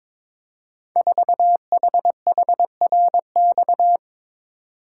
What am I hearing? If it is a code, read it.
Morse code, heard as a keyed tone, about 22 wpm, 700 Hz. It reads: 4HHRX
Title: Code Groups - 5 Letter:Numbers - 22wpm